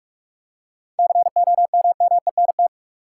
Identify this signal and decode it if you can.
Morse code, heard as a keyed tone, about 46 wpm, 700 Hz. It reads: COMMENT
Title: Seven Letter Words - 45wpm